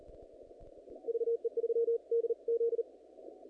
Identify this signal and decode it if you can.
Morse code, heard as a keyed tone, about 36 wpm, 460 Hz. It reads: VE3DZ
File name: contests / naqp-cw-k5tr-op-k5pi- -0100